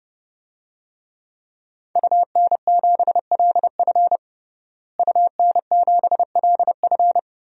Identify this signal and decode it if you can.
Morse code, heard as a keyed tone, about 30 wpm, 700 Hz. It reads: UN7LF UN7LF
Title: DXCC-100 Call Signs - 5 Characters - Repeated 3x - 30wpm